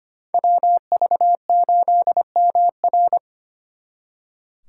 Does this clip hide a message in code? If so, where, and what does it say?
Yes: Morse code from 0.3 to 3.2 s, heard as a keyed tone, about 32 wpm, 700 Hz. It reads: WV8MR